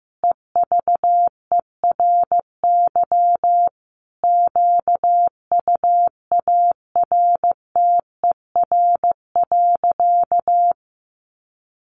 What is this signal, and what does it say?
Morse code, heard as a keyed tone, about 15 wpm, 700 Hz. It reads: EVERY QUARTER.
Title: Sentences from Top 1500 Words 15wpm